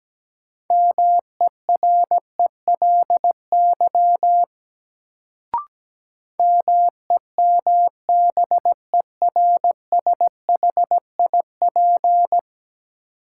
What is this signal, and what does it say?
Morse code, heard as a keyed tone, about 17 wpm, 700 Hz. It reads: MERELY MEMBERSHIP
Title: Sentences from Top 1400 Words - New Word Drill - 17wpm